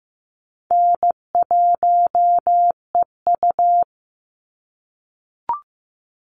Morse code, heard as a keyed tone, about 15 wpm, 700 Hz. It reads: N1EU
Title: NA SOTA Chaser Callsigns 15wpm